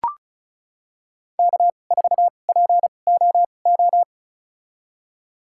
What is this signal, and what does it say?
Morse code, heard as a keyed tone, about 34 wpm, spaced as for 25 wpm, 700 Hz. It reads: K4POO
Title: USA Call Signs - 5 Characters - 2x Character Spacing - 35wpm